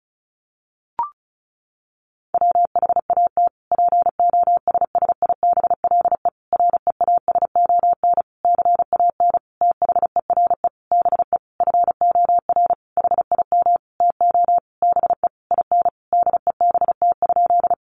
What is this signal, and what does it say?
Morse code, heard as a keyed tone, about 34 wpm, 700 Hz. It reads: WHAT POSSIBLE REASON CAN THERE BE FOR HIM TO BE IN DEBT?
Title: Sentences from Top 1300 Words 35wpm